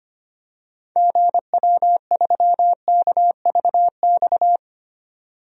Morse code, heard as a keyed tone, about 25 wpm, 700 Hz. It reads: GW3KVX
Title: DXCC-100 Call Signs - 6 Characters - 25wpm